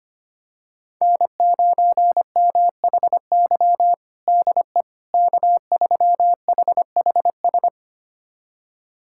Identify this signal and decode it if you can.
Morse code, heard as a keyed tone, about 25 wpm, 700 Hz. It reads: N9MHY DE K3HHS
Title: USA Callsign DE Callsign - 5 Characters - 25wpm